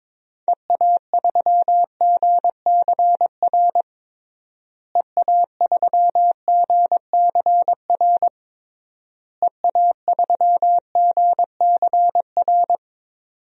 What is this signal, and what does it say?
Morse code, heard as a keyed tone, about 22 wpm, 700 Hz. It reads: EA3GCR EA3GCR EA3GCR
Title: DXCC-100 Call Signs - 6 Characters - Repeated 3x - 22wpm